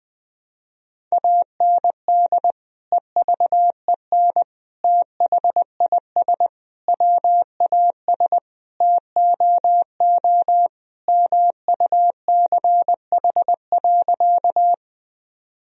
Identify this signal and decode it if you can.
Morse code, heard as a keyed tone, about 20 wpm, 700 Hz. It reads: AND EVEN THIS WAS TOO MUCH.
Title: Sentences from Top 1000 Words - Review Part 1 - 20wpm